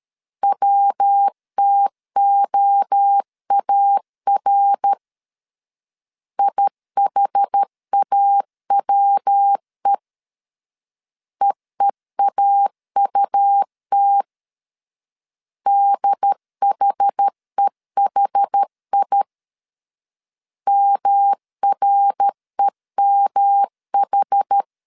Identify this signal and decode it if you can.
Morse code, heard as a keyed tone, about 13 wpm, 800 Hz. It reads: WTOAR IHAWE EEAUT DHEHI MREMH
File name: CW / DET / 1 3 1500C S C 12wpm 2750